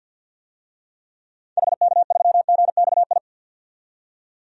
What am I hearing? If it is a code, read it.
Morse code, heard as a keyed tone, about 47 wpm, 700 Hz. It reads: HK3GXI
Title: DXCC-100 Call Signs - 6 Characters - 50wpm